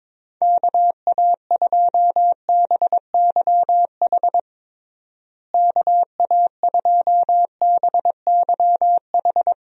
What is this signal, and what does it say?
Morse code, heard as a keyed tone, about 22 wpm, 700 Hz. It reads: KA2BYH KA2BYH
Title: USA Call Signs - 6 Characters - Repeated 2x - 22wpm